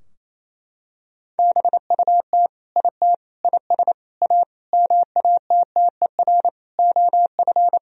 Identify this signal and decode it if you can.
Morse code, heard as a keyed tone, about 28 wpm, 700 Hz. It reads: BUT IT IS A MATTER OF
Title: Sentences from Top 1000 Words - Review Part 3 - 28wpm